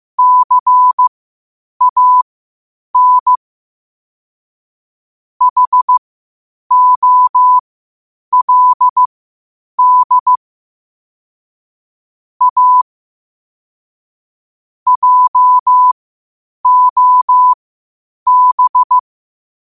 Morse code, heard as a keyed tone, about 15 wpm, spaced as for 8 wpm, 1000 Hz. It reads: CAN HOLD A JOB
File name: qotdfiles / 8wpm